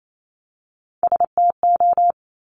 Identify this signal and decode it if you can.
Morse code, heard as a keyed tone, about 28 wpm, 700 Hz. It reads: STO
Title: Top Three Letter Combinations in Words 28wpm